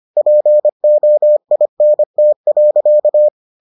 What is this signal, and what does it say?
Morse code, heard as a keyed tone, about 25 wpm, 600 Hz. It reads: POINT.